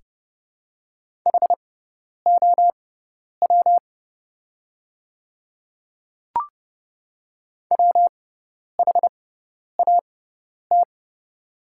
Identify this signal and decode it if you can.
Morse code, heard as a keyed tone, about 30 wpm, spaced as for 10 wpm, 700 Hz. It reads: HOW WHAT
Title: 6x Character Spacing - Top 100 Words - 30wpm